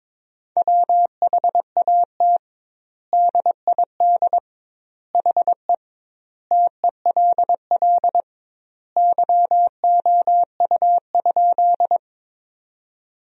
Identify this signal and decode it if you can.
Morse code, heard as a keyed tone, about 22 wpm, 700 Hz. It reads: WHAT DID HE TELL YOU?
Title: 2x Word-Spacing - Sentences from Top 200 Words - 22wpm